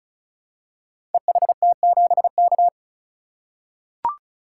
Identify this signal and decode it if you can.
Morse code, heard as a keyed tone, about 36 wpm, 700 Hz. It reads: EHT7K
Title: Code Groups - 5 Letter:Numbers - 35wpm